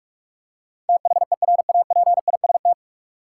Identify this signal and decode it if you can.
Morse code, heard as a keyed tone, about 46 wpm, 700 Hz. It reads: THERAPIST
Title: K1USN SST QSO Practice 45wpm